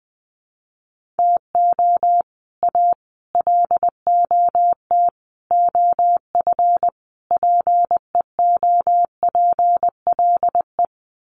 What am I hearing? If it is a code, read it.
Morse code, heard as a keyed tone, about 20 wpm, 700 Hz. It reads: TO A LOT OF PEOPLE
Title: Sets of 5 Words Encore 20wpm